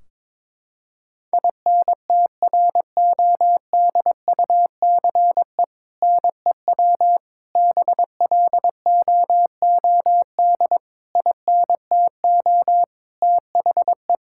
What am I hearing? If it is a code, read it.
Morse code, heard as a keyed tone, about 22 wpm, 700 Hz. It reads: INTRODUCE NEW BLOOD INTO THE
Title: Sentences from Top 1600 Words 22wpm